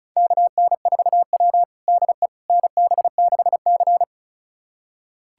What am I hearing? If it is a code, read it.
Morse code, heard as a keyed tone, about 36 wpm, 700 Hz. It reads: KN4W DE NB6C
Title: USA Callsign DE Callsign - 4 Characters - 35wpm